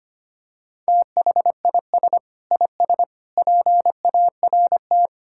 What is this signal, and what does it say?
Morse code, heard as a keyed tone, about 25 wpm, 700 Hz. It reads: THIS IS PART
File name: practice / 25wpm / Sentences with 2-Word Phrases - 25wpm